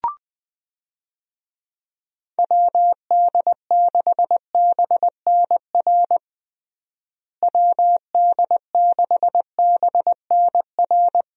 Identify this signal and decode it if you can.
Morse code, heard as a keyed tone, about 20 wpm, 700 Hz. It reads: WD6BNR WD6BNR
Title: USA Call Signs - 6 Characters - Repeated 2x - 20wpm